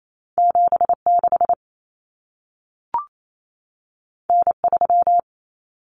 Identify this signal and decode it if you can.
Morse code, heard as a keyed tone, about 28 wpm, 700 Hz. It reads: 76 N3
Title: Common Cut Numbers - 2-Digit 28wpm